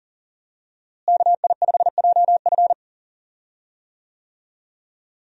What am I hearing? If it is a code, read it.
Morse code, heard as a keyed tone, about 40 wpm, 700 Hz. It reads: KI5JF
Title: USA Call Signs - 5 Characters - Repeated 3x - 40wpm